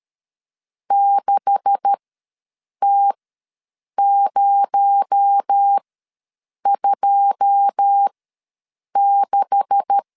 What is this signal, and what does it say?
Morse code, heard as a keyed tone, about 13 wpm, spaced as for 7 wpm, 800 Hz. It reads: 6T026
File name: LEZIONI / 9 286 ws 12wpm 3351